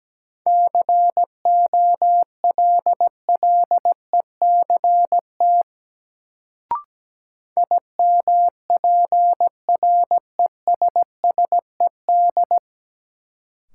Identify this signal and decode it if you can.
Morse code, heard as a keyed tone, about 17 wpm, 700 Hz. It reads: COLLECT IMPRESSED
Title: Sentences from Top 1600 Words - New Word Drill - 17wpm